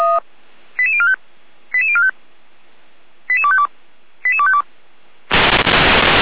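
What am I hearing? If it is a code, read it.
Five-tone selective call (ZVEI), heard as quick runs of five tones, tones of about 80 ms. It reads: 88045 88045 88252 88252
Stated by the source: ZVEI five-tone call 88247 received by SDR, logged by Feuerwehr Orlamuende (f85360).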